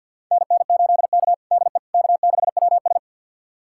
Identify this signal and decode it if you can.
Morse code, heard as a keyed tone, about 51 wpm, 700 Hz. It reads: NN8K DE K6WS
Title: USA Callsign DE Callsign - 4 Characters - 50wpm